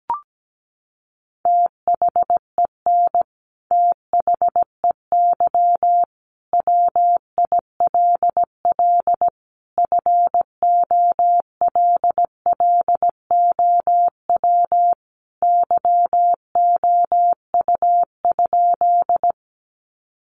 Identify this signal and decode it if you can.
Morse code, heard as a keyed tone, about 17 wpm, 700 Hz. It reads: THEN THEY WILL FOLLOW YOU?
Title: Sentences from Top 200 Words 17wpm